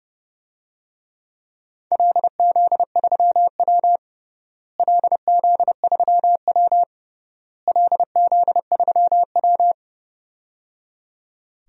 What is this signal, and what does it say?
Morse code, heard as a keyed tone, about 28 wpm, 700 Hz. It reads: LZ3W LZ3W LZ3W
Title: DXCC-100 Call Signs - 4 Characters - Repeated 3x - 30wpm